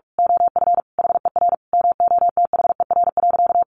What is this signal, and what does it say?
Morse code, heard as a keyed tone, about 44 wpm, 700 Hz. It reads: OF HER MOTHER.
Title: Sentences from Top 1100 Words 45wpm